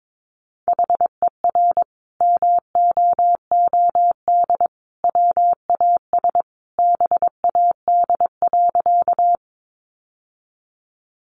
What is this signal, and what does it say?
Morse code, heard as a keyed tone, about 22 wpm, 700 Hz. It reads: HER MOOD WAS BAD.
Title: Sentences from Top 1100 Words 22wpm